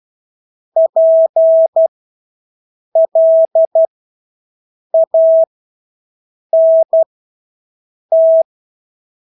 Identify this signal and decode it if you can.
Morse code, heard as a keyed tone, about 12 wpm, spaced as for 6 wpm, 650 Hz. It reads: PLANT
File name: Engelske ord 12-6 wpm